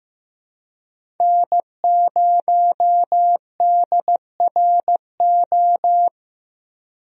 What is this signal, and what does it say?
Morse code, heard as a keyed tone, about 15 wpm, 700 Hz. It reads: N0DRO
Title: USA Call Signs - 5 Characters - Repeated 3x - 15wpm